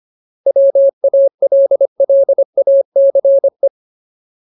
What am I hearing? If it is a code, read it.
Morse code, heard as a keyed tone, about 25 wpm, 550 Hz. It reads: WALLACE
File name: Ultimate Word List / 25WPM Ultimate